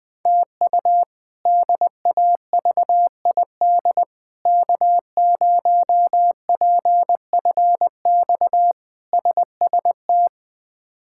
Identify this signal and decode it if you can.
Morse code, heard as a keyed tone, about 20 wpm, 700 Hz. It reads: TU DAVID K0PFX SST
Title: K1USN SST QSO Practice - 20wpm